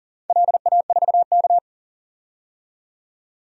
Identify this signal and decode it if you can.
Morse code, heard as a keyed tone, about 37 wpm, 700 Hz. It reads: LA4K
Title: DXCC-100 Call Signs - 4 Characters - 40wpm